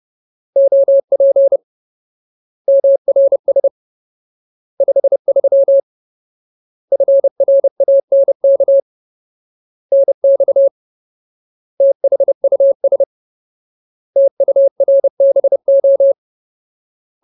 Morse code, heard as a keyed tone, about 30 wpm, 550 Hz. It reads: OP MRS 53 FRANK NX THUS TURBO
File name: Ultimate Word List / 30WPM Ultimate